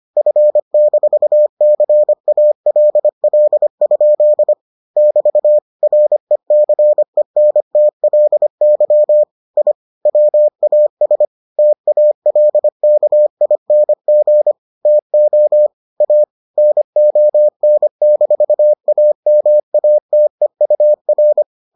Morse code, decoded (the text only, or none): F-CALL? <BT> RECENTLY I WAS TALKING TO A NON-AMATEUR